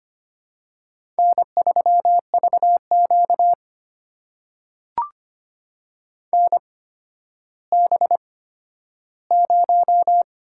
Morse code, heard as a keyed tone, about 25 wpm, spaced as for 12 wpm, 700 Hz. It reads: N3VQ NB0
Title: USA Call Signs - 4 Characters - 8x Character Spacing - 25wpm